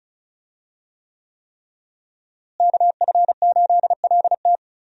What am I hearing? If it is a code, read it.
Morse code, heard as a keyed tone, about 34 wpm, 700 Hz. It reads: KF8LT
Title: USA Call Signs - 5 Characters - Repeated 3x - 35wpm